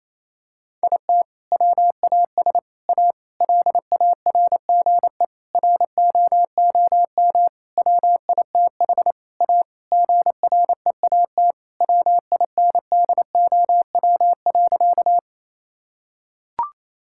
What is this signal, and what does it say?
Morse code, heard as a keyed tone, about 27 wpm, 700 Hz. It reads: IT WAS A LARGE ROOM WITH A GREAT WINDOW.
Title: Sentences from Top 1600 Words 28wpm